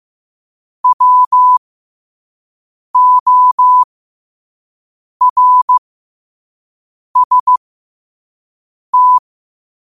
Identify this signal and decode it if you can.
Morse code, heard as a keyed tone, about 15 wpm, spaced as for 5 wpm, 1000 Hz. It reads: WORST
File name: qotdfiles / 5wpm